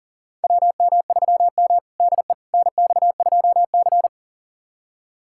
Morse code, heard as a keyed tone, about 40 wpm, 700 Hz. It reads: WM3M DE NX2C
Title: USA Callsign DE Callsign - 4 Characters - 40wpm